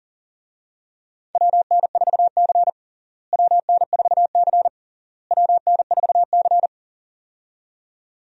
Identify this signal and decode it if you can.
Morse code, heard as a keyed tone, about 40 wpm, 700 Hz. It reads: WN4C WN4C WN4C
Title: USA Call Signs - 4 Characters - Repeated 3x - 40wpm